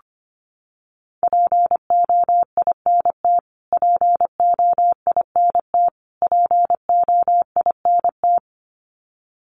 Morse code, heard as a keyed tone, about 25 wpm, 700 Hz. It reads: POINT POINT POINT
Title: Top 200 Words Repeated 25wpm